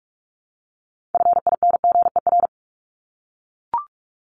Morse code, heard as a keyed tone, about 44 wpm, 700 Hz. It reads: FINGER
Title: Top 1501-1750 Words 45wpm